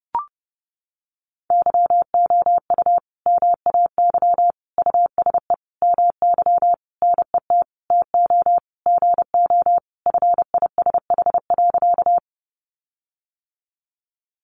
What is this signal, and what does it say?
Morse code, heard as a keyed tone, about 30 wpm, 700 Hz. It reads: YOU MAY USE MY NET TO GO FISH.
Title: Sentences from Top 1400 Words 30wpm